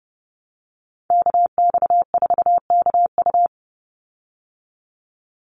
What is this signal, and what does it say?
Morse code, heard as a keyed tone, about 30 wpm, 700 Hz. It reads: KX4KU
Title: Call Signs Repeated 2x - 30wpm